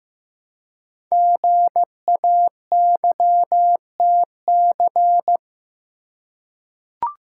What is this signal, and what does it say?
Morse code, heard as a keyed tone, about 15 wpm, 700 Hz. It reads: GAYTC
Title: Code Groups - 5 Letters - 15wpm